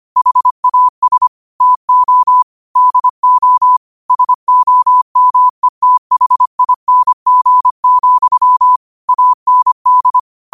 Morse code, heard as a keyed tone, about 25 wpm, 1000 Hz. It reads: HAS TO DO SOMETHING, AND
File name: qotdfiles / 25wpm